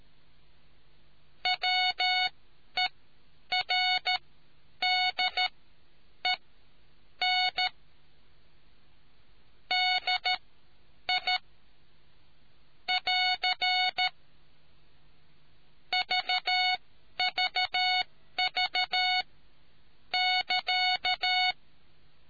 Morse code, heard as a keyed tone, about 13 wpm, spaced as for 9 wpm, 740 Hz. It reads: WERDEN DI <AR> VVV <KA>